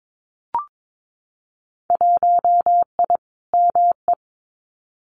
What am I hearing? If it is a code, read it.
Morse code, heard as a keyed tone, about 22 wpm, 700 Hz. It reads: 1I ME
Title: Winter Field Day Exchanges 22wpm